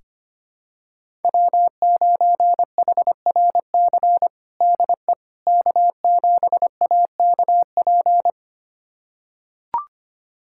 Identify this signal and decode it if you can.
Morse code, heard as a keyed tone, about 25 wpm, 700 Hz. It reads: W9HRC DE K7AKP
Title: USA Callsign DE Callsign - 5 Characters - 25wpm